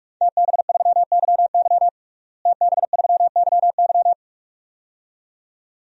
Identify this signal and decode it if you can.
Morse code, heard as a keyed tone, about 46 wpm, 700 Hz. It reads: TB3YY TB3YY
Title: DXCC-100 Call Signs - 5 Characters - Repeated 2x - 45wpm